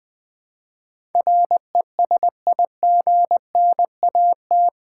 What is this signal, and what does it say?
Morse code, heard as a keyed tone, about 20 wpm, 700 Hz. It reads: RESIGNAT
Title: Build A Word - 4 Letters At A Time - 11-Letter Words - 20wpm